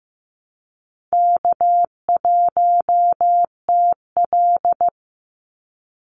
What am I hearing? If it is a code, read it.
Morse code, heard as a keyed tone, about 15 wpm, 700 Hz. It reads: K1TL
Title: Easy Call Signs Repeated 2x - 15wpm